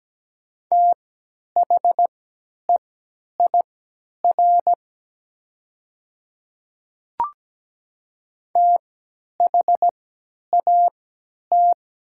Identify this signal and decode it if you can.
Morse code, heard as a keyed tone, about 17 wpm, spaced as for 10 wpm, 700 Hz. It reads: THEIR THAT
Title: 3x Character Spacing - Top 100 Words - 17wpm